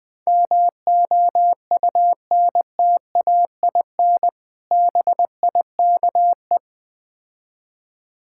Morse code, heard as a keyed tone, about 20 wpm, 690 Hz. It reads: MOUNTAIN BIKE
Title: Difficult Words with Hints - 20wpm